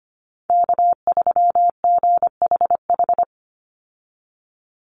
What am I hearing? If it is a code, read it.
Morse code, heard as a keyed tone, about 25 wpm, 700 Hz. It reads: K3GHH
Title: Call Signs - 25wpm